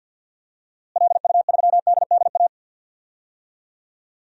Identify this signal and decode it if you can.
Morse code, heard as a keyed tone, about 47 wpm, 700 Hz. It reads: LU3DDA
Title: DXCC-100 Call Signs - 6 Characters - Repeated 2x - 50wpm